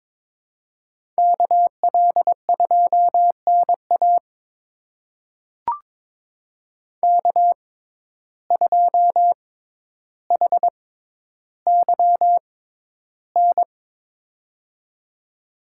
Morse code, heard as a keyed tone, about 22 wpm, spaced as for 11 wpm, 700 Hz. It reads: KL2NA K2HYN
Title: USA Call Signs - 5 Characters - 6x Character Spacing - 22wpm